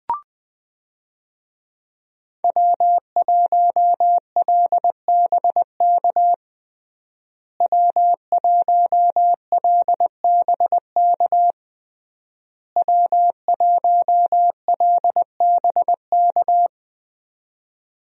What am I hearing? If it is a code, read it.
Morse code, heard as a keyed tone, about 20 wpm, 700 Hz. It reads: W1LBK W1LBK W1LBK
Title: USA Call Signs - 5 Characters - Repeated 3x - 20wpm